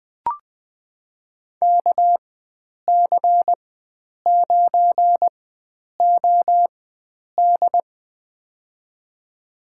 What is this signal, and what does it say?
Morse code, heard as a keyed tone, about 20 wpm, spaced as for 9 wpm, 700 Hz. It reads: KC9OD
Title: USA Call Signs - 5 Characters - 4x Character Spacing - 20wpm